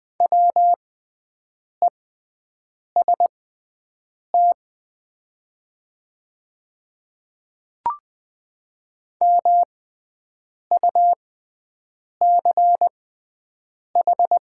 Morse code, heard as a keyed tone, about 20 wpm, spaced as for 7 wpm, 700 Hz. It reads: WEST MUCH
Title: 6x Character Spacing - Top 500 Words - 20wpm